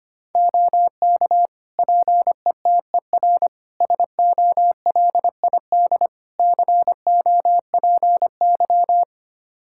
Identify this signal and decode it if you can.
Morse code, heard as a keyed tone, about 25 wpm, 700 Hz. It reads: OK PETER SOLID COPY
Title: K1USN SST QSO Practice 25wpm